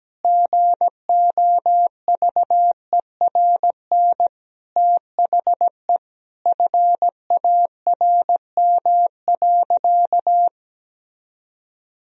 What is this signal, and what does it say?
Morse code, heard as a keyed tone, about 17 wpm, 700 Hz. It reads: GOVERN THE FARM.
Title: Sentences from Top 1000 Words - Review Part 2 - 17wpm